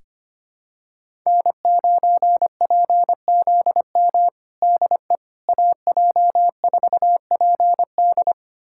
Morse code, heard as a keyed tone, about 25 wpm, 700 Hz. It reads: N9PZM DE AJ4PD
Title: USA Callsign DE Callsign - 5 Characters - 25wpm